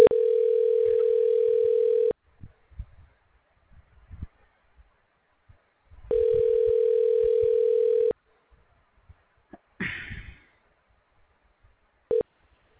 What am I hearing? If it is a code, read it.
Ringback tone (North America), 440 + 480 Hz.